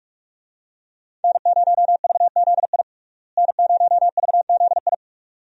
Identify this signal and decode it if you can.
Morse code, heard as a keyed tone, about 46 wpm, 700 Hz. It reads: N0VZI N0VZI
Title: USA Call Signs - 5 Characters - Repeated 2x - 45wpm